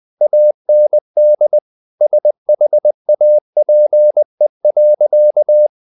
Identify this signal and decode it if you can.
Morse code, heard as a keyed tone, about 20 wpm, 600 Hz. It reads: AND SHAPE.
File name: the old man and the sea / the old man and the sea part 1 20wpm